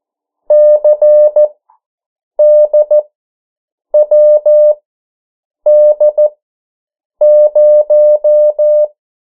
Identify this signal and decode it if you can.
Morse code, heard as a keyed tone, about 14 wpm, spaced as for 7 wpm, 600 Hz. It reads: CDWD0
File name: morse mp3 / 09 urok 8